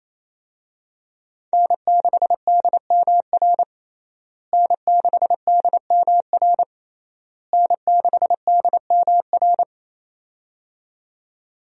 Morse code, heard as a keyed tone, about 28 wpm, 700 Hz. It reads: N6DMR N6DMR N6DMR
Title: USA Call Signs - 5 Characters - Repeated 3x - 28wpm